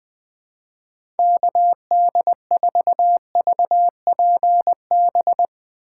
Morse code, heard as a keyed tone, about 20 wpm, 700 Hz. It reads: KD4VPB
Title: Speed Racing - USA 2x3 Call Signs 20wpm